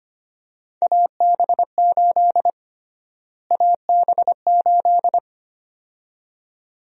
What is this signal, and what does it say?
Morse code, heard as a keyed tone, about 25 wpm, 700 Hz. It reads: AB8 AB8
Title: USA Call Sign Prefixes - Repeated 2x - 25wpm